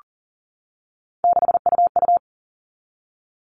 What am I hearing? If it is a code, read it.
Morse code, heard as a keyed tone, about 40 wpm, 700 Hz. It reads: 6UU
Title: All Cut Numbers - 3-Digit 40wpm